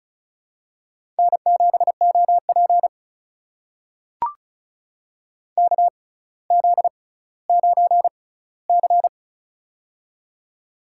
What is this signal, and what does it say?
Morse code, heard as a keyed tone, about 36 wpm, spaced as for 18 wpm, 700 Hz. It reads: N7OP KZ9C
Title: USA Call Signs - 4 Characters - 6x Character Spacing - 35wpm